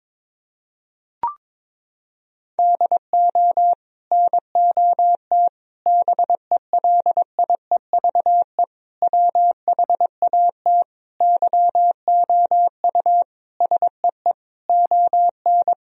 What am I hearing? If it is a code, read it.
Morse code, heard as a keyed tone, about 22 wpm, 700 Hz. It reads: DO NOT BELIEVE WHAT YOU SEE ON
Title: Sentences from Top 1000 Words - Review Part 4 - 22wpm